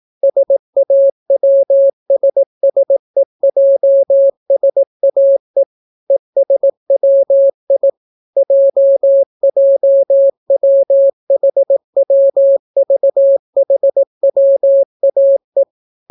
Morse code, heard as a keyed tone, about 18 wpm, 550 Hz. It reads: SAWSSEJSAE ESWI JJWHWVHWAE